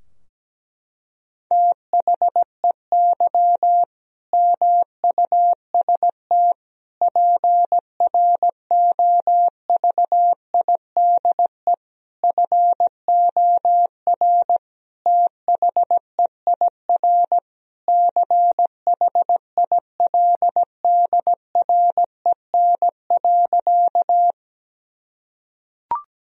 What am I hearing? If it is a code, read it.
Morse code, heard as a keyed tone, about 17 wpm, 700 Hz. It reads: THEY MUST PROVIDE FOR THEIR CHILDREN.
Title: Sentences from Top 1000 Words - Review Part 3 - 17wpm